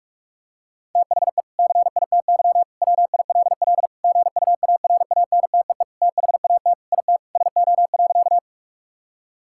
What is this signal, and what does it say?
Morse code, heard as a keyed tone, about 41 wpm, 700 Hz. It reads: THE CITY WILL GUARANTEE THAT IT SO.
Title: Sentences from Top 1600 Words 45wpm